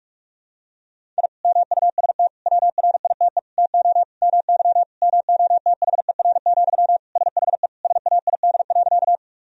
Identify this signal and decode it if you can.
Morse code, heard as a keyed tone, about 42 wpm, 700 Hz. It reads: I MUST WRITE TO MY MOTHER, SHE SAID.
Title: Sentences from Top 1000 Words - Review Part 1 - 45wpm